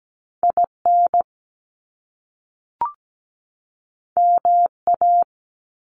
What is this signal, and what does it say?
Morse code, heard as a keyed tone, about 17 wpm, 700 Hz. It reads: IN MA
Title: US States 17wpm